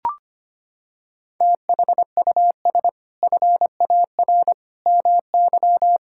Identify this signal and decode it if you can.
Morse code, heard as a keyed tone, about 25 wpm, 700 Hz. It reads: THUS FAR MY
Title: Sentences from Top 1000 Words - Review Part 5 - 25wpm